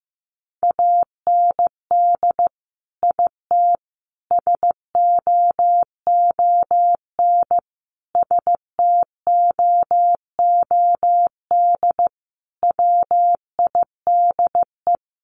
Morse code, heard as a keyed tone, about 15 wpm, 700 Hz. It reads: AND IT SOON STOOD WIDE
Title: Sentences from Top 900 Words 15wpm